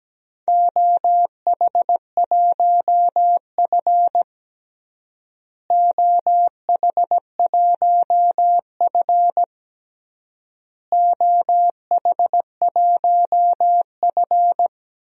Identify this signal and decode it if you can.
Morse code, heard as a keyed tone, about 17 wpm, 700 Hz. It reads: OH1F OH1F OH1F
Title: DXCC-100 Call Signs - 4 Characters - Repeated 3x - 17wpm